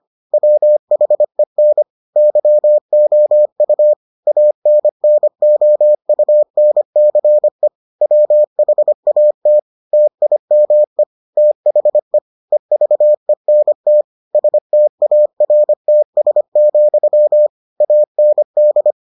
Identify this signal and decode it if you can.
Morse code, heard as a keyed tone, about 25 wpm, 600 Hz. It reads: WHEN YOU ANNOUNCE WHAT TIME THE EVENT STARTS, AND